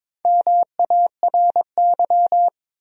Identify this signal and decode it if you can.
Morse code, heard as a keyed tone, about 22 wpm, 700 Hz. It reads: MARY
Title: USA Common Names - Max 4 Letters - 22wpm